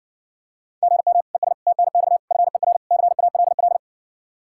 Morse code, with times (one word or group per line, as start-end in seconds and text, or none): 0.8-1.2 GM
1.3-1.5 ES
1.7-2.2 TNX
2.3-2.8 FER
2.9-3.8 CALL